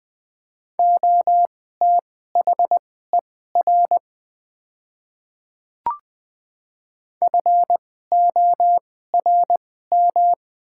Morse code, heard as a keyed tone, about 20 wpm, spaced as for 15 wpm, 700 Hz. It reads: OTHER FORM
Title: 2x Character Spacing - Top 200 Words - 20wpm